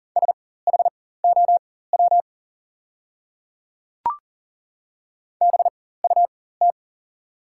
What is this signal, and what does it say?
Morse code, heard as a keyed tone, about 40 wpm, spaced as for 19 wpm, 690 Hz. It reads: SHOW BUT
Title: 4x Character Spacing - Top 300 Words - 40wpm